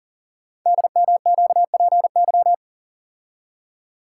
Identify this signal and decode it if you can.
Morse code, heard as a keyed tone, about 41 wpm, 700 Hz. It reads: DMQPY
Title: Code Groups - 5 Letters - 40wpm